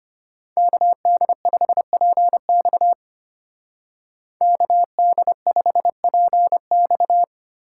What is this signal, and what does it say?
Morse code, heard as a keyed tone, about 34 wpm, 700 Hz. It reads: KD5PX KD5PX